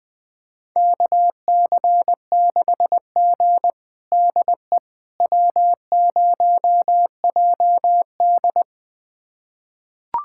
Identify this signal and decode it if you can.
Morse code, heard as a keyed tone, about 21 wpm, 700 Hz. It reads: KC6G DE W0JD
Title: USA Callsign DE Callsign - 4 Characters - 20wpm